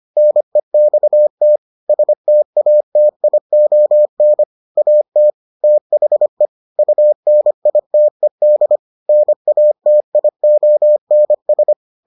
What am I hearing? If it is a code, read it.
Morse code, heard as a keyed tone, about 25 wpm, 600 Hz. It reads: NEXT STATION AT THE UNITED NATIONS